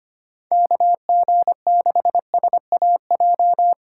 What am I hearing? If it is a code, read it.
Morse code, heard as a keyed tone, about 30 wpm, 700 Hz. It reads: KG6SAJ
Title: Speed Racing - USA 2x3 Call Signs 25wpm